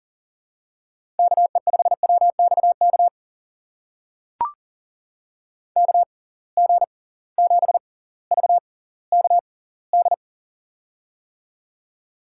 Morse code, heard as a keyed tone, about 40 wpm, spaced as for 20 wpm, 700 Hz. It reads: KE5WXK KG7VKD
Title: USA Call Signs - 6 Characters - 6x Character Spacing - 40wpm